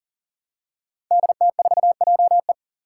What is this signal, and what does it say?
Morse code, heard as a keyed tone, about 40 wpm, 700 Hz. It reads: DT4JE
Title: Five Challenging Letter-Number Combinations 40wpm